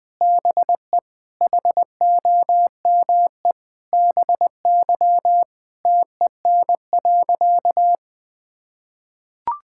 Morse code, heard as a keyed tone, about 20 wpm, 700 Hz. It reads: BE HOME BY TEN.
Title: Sentences with 4-Word Phrases - 20wpm